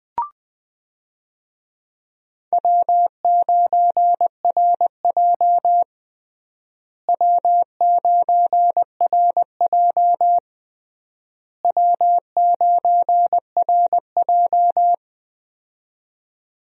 Morse code, heard as a keyed tone, about 20 wpm, 700 Hz. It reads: W9RJ W9RJ W9RJ
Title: USA Call Signs - 4 Characters - Repeated 3x - 20wpm